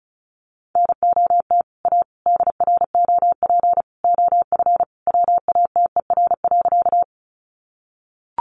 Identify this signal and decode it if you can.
Morse code, heard as a keyed tone, about 34 wpm, 700 Hz. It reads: NOT A DROP OF WATER.
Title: Sentences from Top 1400 Words 35wpm